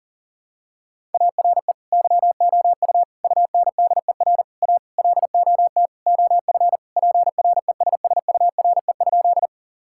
Morse code, heard as a keyed tone, about 40 wpm, 700 Hz. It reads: ARE YOU UNDER A LOT OF PRESSURE?
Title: Sentences from Top 1000 Words - Review Part 6 - 40wpm